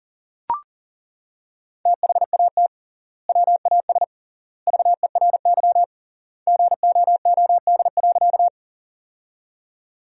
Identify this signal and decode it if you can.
Morse code, heard as a keyed tone, about 40 wpm, 700 Hz. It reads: THAT WAS VERY GOOD.
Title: 3x Word-Spacing - Sentences from Top 200 Words - 40wpm